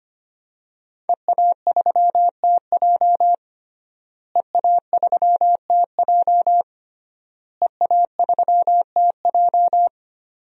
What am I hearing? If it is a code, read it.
Morse code, heard as a keyed tone, about 25 wpm, 700 Hz. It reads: EA3TJ EA3TJ EA3TJ
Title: DXCC-100 Call Signs - 5 Characters - Repeated 3x - 25wpm